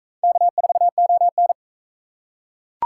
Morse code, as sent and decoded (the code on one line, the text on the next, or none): -.- ....- --- -.
K4ON